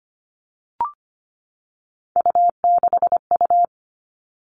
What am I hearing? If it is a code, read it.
Morse code, heard as a keyed tone, about 25 wpm, 700 Hz. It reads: U6U
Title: All Cut Numbers - 3-Digit 25wpm